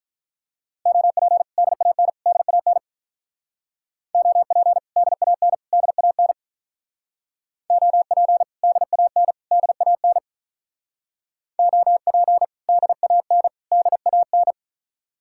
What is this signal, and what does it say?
Morse code, heard as a keyed tone, about 43 wpm, 700 Hz. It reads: OP DAN DAN OP DAN DAN OP DAN DAN OP DAN DAN
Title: Speed-Racing Common QSO Phrases Encore - 35wpm